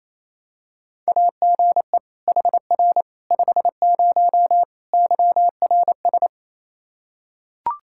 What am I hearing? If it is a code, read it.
Morse code, heard as a keyed tone, about 28 wpm, 700 Hz. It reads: AGE HR 50 YRS
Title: K1USN SST QSO Practice 28wpm